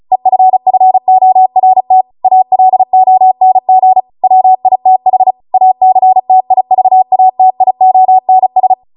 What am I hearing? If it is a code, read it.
Morse code, heard as a keyed tone, about 36 wpm, 750 Hz. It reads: EFFORT ALONG WITH ACTIVATIONS